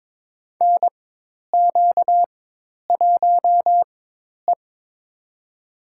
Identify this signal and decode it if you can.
Morse code, heard as a keyed tone, about 23 wpm, spaced as for 10 wpm, 700 Hz. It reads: NQ1E
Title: USA Call Signs - 4 Characters - 4x Character Spacing - 22wpm